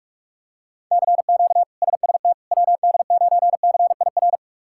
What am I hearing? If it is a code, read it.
Morse code, heard as a keyed tone, about 46 wpm, 700 Hz. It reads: CQ SST WD9CIR
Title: K1USN SST QSO Practice - 45wpm